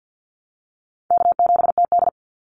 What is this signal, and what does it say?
Morse code, heard as a keyed tone, about 50 wpm, 700 Hz. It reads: K7TD
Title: Easy Call Signs - 50wpm